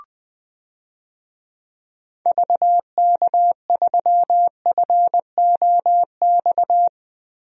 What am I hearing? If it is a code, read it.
Morse code, heard as a keyed tone, about 21 wpm, 700 Hz. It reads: VK3FOX
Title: DXCC-100 Call Signs - 6 Characters - Repeated 3x - 20wpm